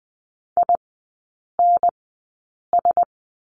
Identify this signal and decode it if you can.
Morse code, heard as a keyed tone, about 20 wpm, spaced as for 8 wpm, 700 Hz. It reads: INS
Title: 041 Lesson 8e - Number 1 - Three Characters 20wpm